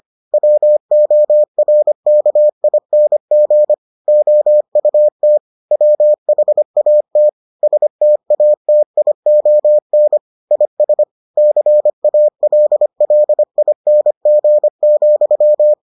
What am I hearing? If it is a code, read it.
Morse code, heard as a keyed tone, about 25 wpm, 600 Hz. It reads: WORKING OUT WHAT STATION IS CALLING,